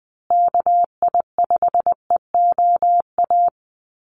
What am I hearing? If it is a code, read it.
Morse code, heard as a keyed tone, about 20 wpm, 700 Hz. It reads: KI5EOA
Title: USA Call Signs Repeated 3x 20wpm